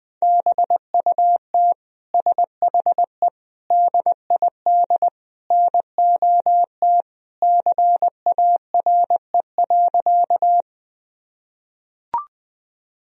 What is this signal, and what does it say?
Morse code, heard as a keyed tone, about 20 wpm, 700 Hz. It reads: BUT SHE DID NOT CARE.
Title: Sentences from Top 1000 Words - Review Part 1 - 20wpm